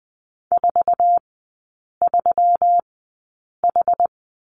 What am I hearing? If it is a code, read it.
Morse code, heard as a keyed tone, about 20 wpm, spaced as for 8 wpm, 700 Hz. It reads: 43H
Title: 166 Lesson 27e - Number 7 - Three Characters 20wpm